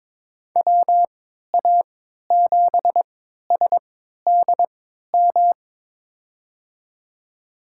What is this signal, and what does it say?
Morse code, heard as a keyed tone, about 22 wpm, spaced as for 13 wpm, 700 Hz. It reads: WA7SDM
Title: USA Call Signs - 6 Characters - 3x Character Spacing - 22wpm